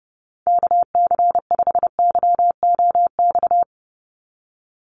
Morse code, heard as a keyed tone, about 28 wpm, 700 Hz. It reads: KC5YOX
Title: USA Call Signs 30wpm